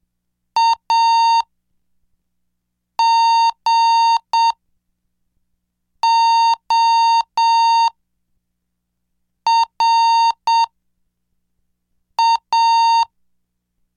Morse code, heard as a keyed tone, about 7 wpm, spaced as for 4 wpm, 920 Hz. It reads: AGORA
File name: cursos / CDestudo / 11 Track 11